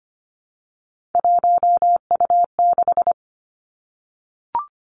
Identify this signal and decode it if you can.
Morse code, heard as a keyed tone, about 25 wpm, 700 Hz. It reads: 1U6
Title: Three Challenging Letter-Number Combinations 25wpm